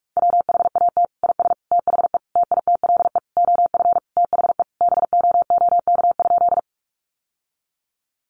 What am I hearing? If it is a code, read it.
Morse code, heard as a keyed tone, about 44 wpm, 700 Hz. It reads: WHAT IS THE TITLE OF THE BOOK?
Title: Sentences from Top 1200 Words 45wpm